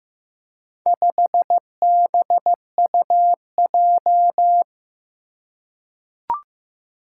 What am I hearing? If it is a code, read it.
Morse code, heard as a keyed tone, about 15 wpm, 700 Hz. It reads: 5BUJ
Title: Four Challenging Letter-Number Combinations 15wpm